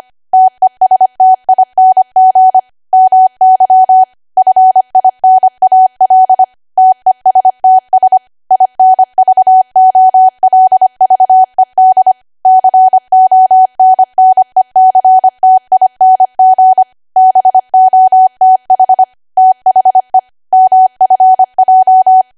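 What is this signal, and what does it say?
Morse code, heard as a keyed tone, about 25 wpm, 750 Hz. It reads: TESTING MY FINAL TESTS INVOLVED CONNECTING BOTH THE MFJ
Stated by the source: ARRL 25 WPM code practice archive.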